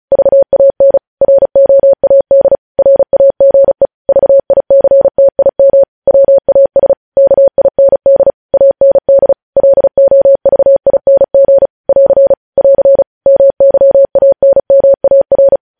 Morse code, heard as a keyed tone, about 36 wpm, 550 Hz. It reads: VAN ROAD RAGE VICTIM WAS KIND AND LOVING <AR> <AR> MYANMAR
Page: News headlines in Morse code at 35 wpm